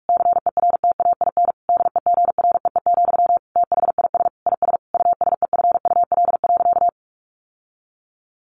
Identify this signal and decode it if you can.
Morse code, heard as a keyed tone, about 44 wpm, 700 Hz. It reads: CERTAIN DEGREE, THIS IS USEFUL.
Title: Sentences from Top 1500 Words 45wpm